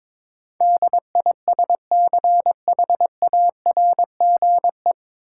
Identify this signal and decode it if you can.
Morse code, heard as a keyed tone, about 22 wpm, 700 Hz. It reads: DISCHARGE
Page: Morse Code Ninja practice 22wpm